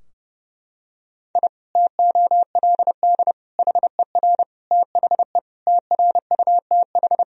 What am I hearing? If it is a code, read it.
Morse code, heard as a keyed tone, about 30 wpm, 700 Hz. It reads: I TOLD HER THE TRUTH
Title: Sentences from Top 1000 Words - Review Part 3 - 30wpm